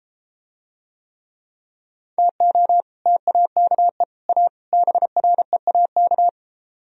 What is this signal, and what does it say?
Morse code, heard as a keyed tone, about 32 wpm, 700 Hz. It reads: TO TAKE A BREAK